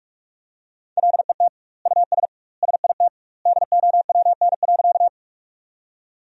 Morse code, heard as a keyed tone, about 46 wpm, 700 Hz. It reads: LET US SIT DOWN.
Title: 2x Word-Spacing - Sentences from Top 500 Words - 45wpm